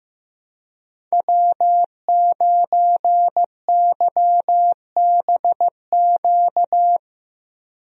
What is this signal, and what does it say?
Morse code, heard as a keyed tone, about 15 wpm, 700 Hz. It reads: W9YBQ
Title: USA Call Signs - 5 Characters - 15wpm